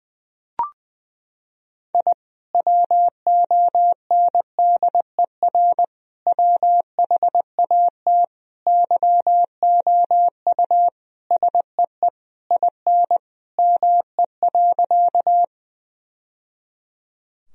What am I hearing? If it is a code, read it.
Morse code, heard as a keyed tone, about 20 wpm, 700 Hz. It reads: I WONDER WHAT YOU SEE IN ME.
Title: Sentences from Top 1000 Words - Review Part 3 - 20wpm